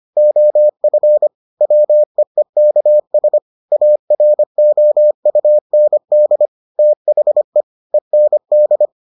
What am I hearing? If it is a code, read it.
Morse code, heard as a keyed tone, about 25 wpm, 600 Hz. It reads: OF WEEKS AROUND THE END